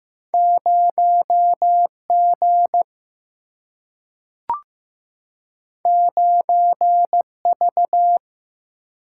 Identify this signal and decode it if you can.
Morse code, heard as a keyed tone, about 15 wpm, 700 Hz. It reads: 0G 9V
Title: Single-Letter Single-Number 15wpm